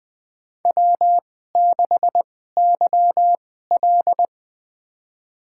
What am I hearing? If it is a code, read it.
Morse code, heard as a keyed tone, about 20 wpm, spaced as for 14 wpm, 700 Hz. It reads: W6YL
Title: USA Call Signs - 2x Character Spacing - 20wpm